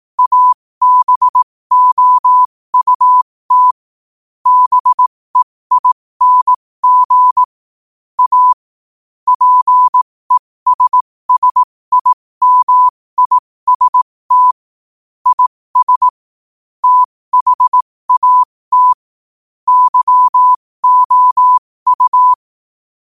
Morse code, heard as a keyed tone, about 18 wpm, spaced as for 15 wpm, 1000 Hz. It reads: ABOUT BEING A PESSIMIST IS THAT YOU
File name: qotdfiles / 15wpm